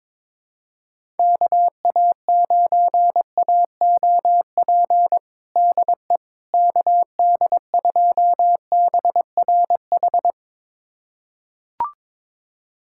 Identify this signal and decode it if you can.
Morse code, heard as a keyed tone, about 22 wpm, 700 Hz. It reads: KA9AOP DE KD2BRH
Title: USA Callsign DE Callsign - 6 Characters - 22wpm